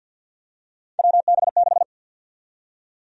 Morse code, heard as a keyed tone, about 53 wpm, 700 Hz. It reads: WB6
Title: USA Call Sign Prefixes - Repeated 2x - 50wpm